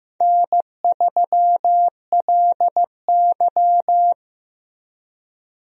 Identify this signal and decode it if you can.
Morse code, heard as a keyed tone, about 15 wpm, 700 Hz. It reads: N3LY
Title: USA Call Signs - 4 Characters - Repeated 2x - 15wpm